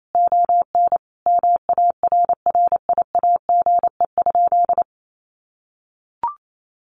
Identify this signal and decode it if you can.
Morse code, heard as a keyed tone, about 28 wpm, 700 Hz. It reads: ON MARRIAGE?
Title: Sentences from Top 1200 Words 28wpm